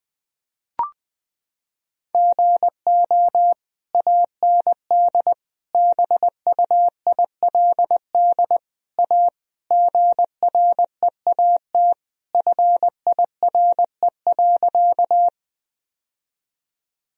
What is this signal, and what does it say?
Morse code, heard as a keyed tone, about 20 wpm, 700 Hz. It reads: GO AND BUILD A GREAT FIRE.
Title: Sentences from Top 1000 Words - Review Part 1 - 20wpm